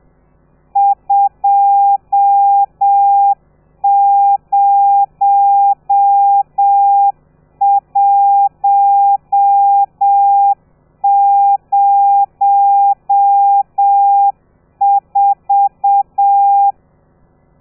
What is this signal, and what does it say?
Morse code, heard as a keyed tone, about 7 wpm, 800 Hz. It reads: 20104
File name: telegrafia / i2tfj / Lez13 30